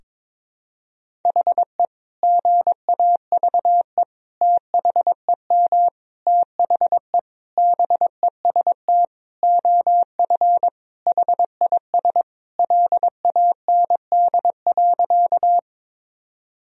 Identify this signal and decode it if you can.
Morse code, heard as a keyed tone, about 22 wpm, 700 Hz. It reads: HE GAVE THEM THE BEST OF HIS LAND.
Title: Sentences from Top 1000 Words - Review Part 1 - 22wpm